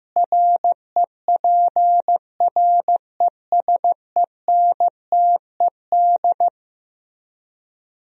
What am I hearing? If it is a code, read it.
Morse code, heard as a keyed tone, about 15 wpm, 700 Hz. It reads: REPRESENTED
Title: Speed-Racing - Sets of 5 Words 15wpm